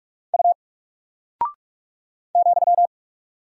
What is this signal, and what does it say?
Morse code, heard as a keyed tone, about 44 wpm, 700 Hz. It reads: U ,
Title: Single Character (Letter:Number:Punctuation:Prosign) 45wpm